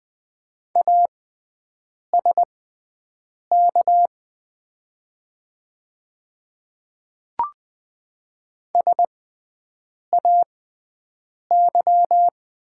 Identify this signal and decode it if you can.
Morse code, heard as a keyed tone, about 20 wpm, spaced as for 7 wpm, 700 Hz. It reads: ASK SAY
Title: 6x Character Spacing - Top 200 Words - 20wpm